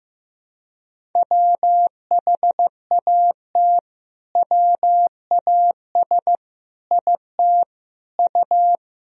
Morse code, heard as a keyed tone, about 15 wpm, 700 Hz. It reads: WHAT WAS IT U
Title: Sentences from Top 200 Words and Abbreviations 15wpm